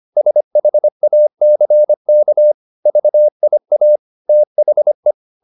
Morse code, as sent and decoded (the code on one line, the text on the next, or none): ... .... .- -.-. -.- / ...- .. .- / - .... .
SHACK VIA THE